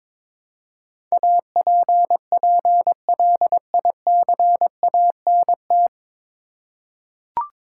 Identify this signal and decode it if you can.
Morse code, heard as a keyed tone, about 23 wpm, 700 Hz. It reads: APPLICANT
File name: 9 Letter Words - First 3 Letters Given - 22wpm